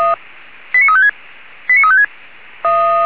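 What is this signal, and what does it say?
Five-tone selective call (ZVEI), heard as quick runs of five tones, tones of about 80 ms. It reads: 89356 89356
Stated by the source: ZVEI five-tone selective call received on an SDR, recorded by Feuerwehr Orlamünde (f85360).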